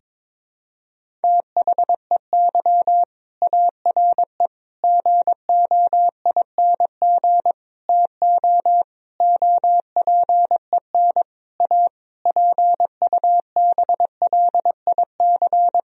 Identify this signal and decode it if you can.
Morse code, heard as a keyed tone, about 22 wpm, 700 Hz. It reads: THEY ARE GOING TO OPEN A PUBLIC
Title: Sentences from Top 1000 Words - Review Part 3 - 22wpm